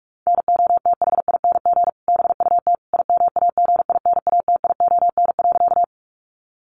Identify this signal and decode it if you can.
Morse code, heard as a keyed tone, about 44 wpm, 700 Hz. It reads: NOTHING BUT IMAGINATION.
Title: Sentences from Top 1300 Words 45wpm